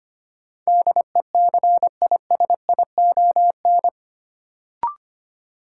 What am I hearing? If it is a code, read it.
Morse code, heard as a keyed tone, about 25 wpm, 700 Hz. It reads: DECISION
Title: Sentences from Top 1000 Words - New Word Drill - 25wpm